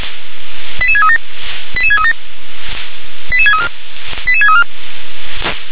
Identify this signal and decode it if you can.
Five-tone selective call (ZVEI), heard as quick runs of five tones, tones of about 70 ms. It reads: 88637 88637 88634 88634